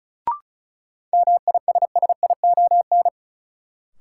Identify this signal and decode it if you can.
Morse code, heard as a keyed tone, about 36 wpm, 700 Hz. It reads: MISSION
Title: Sentences from Top 1600 Words - New Word Drill Alt - 35wpm